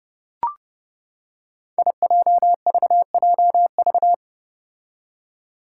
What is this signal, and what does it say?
Morse code, heard as a keyed tone, about 29 wpm, 700 Hz. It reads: IJVJV
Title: Five Challenging Letter-Number Combinations 30wpm